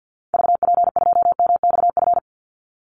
Morse code, heard as a keyed tone, about 50 wpm, 700 Hz. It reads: VP2MXF
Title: Call Signs Repeated 3x - 50wpm